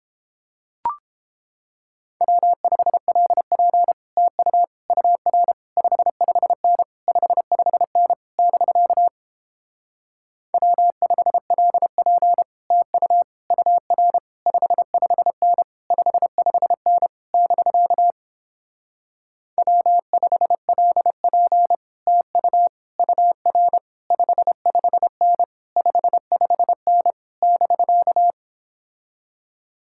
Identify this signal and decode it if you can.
Morse code, heard as a keyed tone, about 31 wpm, 700 Hz. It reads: W5LP TU UR 55N 55N <BK> W5LP TU UR 55N 55N <BK> W5LP TU UR 55N 55N <BK>